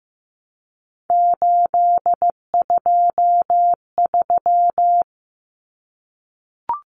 Morse code, heard as a keyed tone, about 15 wpm, 700 Hz. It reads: 823
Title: Common Cut Numbers - 3-Digit 15wpm